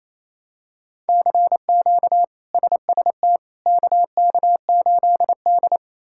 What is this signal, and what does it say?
Morse code, heard as a keyed tone, about 28 wpm, 700 Hz. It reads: CQ SST KK8D
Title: K1USN SST QSO Practice - 28wpm